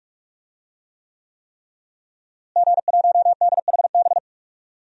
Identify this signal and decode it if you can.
Morse code, heard as a keyed tone, about 46 wpm, 700 Hz. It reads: G1DHB
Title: DXCC-100 Call Signs - 5 Characters - Repeated 3x - 45wpm